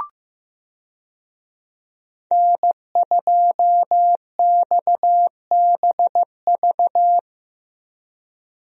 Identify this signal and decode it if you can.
Morse code, heard as a keyed tone, about 15 wpm, 700 Hz. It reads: N2XBV
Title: USA Call Signs - 5 Characters - Repeated 3x - 15wpm